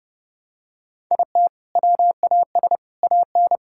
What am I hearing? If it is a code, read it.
Morse code, heard as a keyed tone, about 32 wpm, 700 Hz. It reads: IT WAS AN
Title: Sentences from Top 1000 Words - Review Part 6 - 30wpm